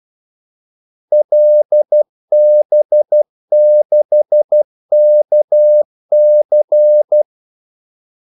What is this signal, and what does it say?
Morse code, heard as a keyed tone, about 12 wpm, 600 Hz. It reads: LB6KC
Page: 2 ekstra ordmellemrum mellem kaldesignalerne